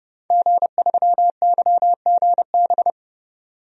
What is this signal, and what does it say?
Morse code, heard as a keyed tone, about 32 wpm, 700 Hz. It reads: G3YGB
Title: DXCC-100 Call Signs - 5 Characters - 30wpm